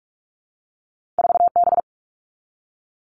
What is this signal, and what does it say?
Morse code, heard as a keyed tone, about 46 wpm, 700 Hz. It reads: 4B
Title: All Cut Numbers - 2-Digit 45wpm